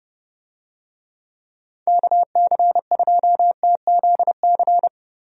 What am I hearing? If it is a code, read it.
Morse code, heard as a keyed tone, about 32 wpm, 700 Hz. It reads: KC2TZC
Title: USA Call Signs - 6 Characters - Repeated 2x - 30wpm